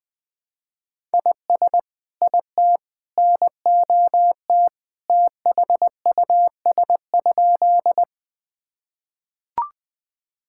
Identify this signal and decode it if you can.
Morse code, heard as a keyed tone, about 20 wpm, 700 Hz. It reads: IS IT NOT THUS?
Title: Sentences from Top 1000 Words - Review Part 5 - 20wpm